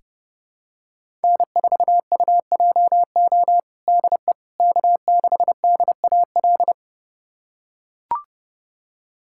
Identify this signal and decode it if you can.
Morse code, heard as a keyed tone, about 30 wpm, 700 Hz. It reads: N4UJO DE K6DAL
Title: USA Callsign DE Callsign - 5 Characters - 30wpm